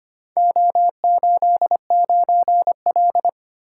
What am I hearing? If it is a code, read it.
Morse code, heard as a keyed tone, about 25 wpm, 700 Hz. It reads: O89L
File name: practice / 25wpm / Four Letter-Number Combinations - 25wpm